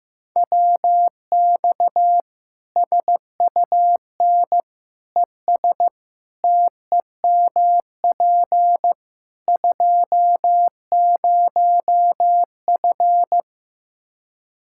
Morse code, heard as a keyed tone, about 15 wpm, 700 Hz. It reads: WX SUN ES TEMP 20F
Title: Practice QSO - Exchange 2 - Info - 15wpm